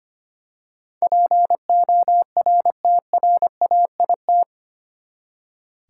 Morse code, heard as a keyed tone, about 25 wpm, 700 Hz. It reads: PORTRAIT
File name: Eight Letter Words - 25wpm